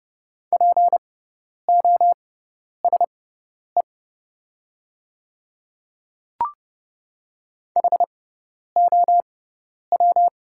Morse code, heard as a keyed tone, about 32 wpm, spaced as for 10 wpm, 700 Hz. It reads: POSE HOW
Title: 6x Character Spacing - Top 500 Words - 30wpm